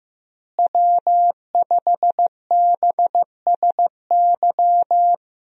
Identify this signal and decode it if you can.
Morse code, heard as a keyed tone, about 16 wpm, 700 Hz. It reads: W5BSY
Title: USA Call Signs - 5 Characters - Repeated 2x - 15wpm